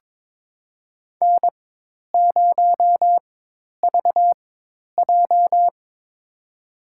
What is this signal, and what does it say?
Morse code, heard as a keyed tone, about 21 wpm, spaced as for 10 wpm, 700 Hz. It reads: N0VJ
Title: USA Call Signs - 4 Characters - 4x Character Spacing - 22wpm